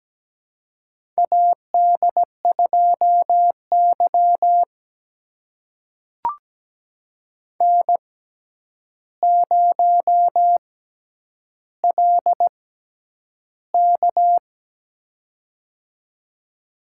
Morse code, heard as a keyed tone, about 17 wpm, spaced as for 9 wpm, 700 Hz. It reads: AD2Y N0LK
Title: USA Call Signs - 4 Characters - 6x Character Spacing - 17wpm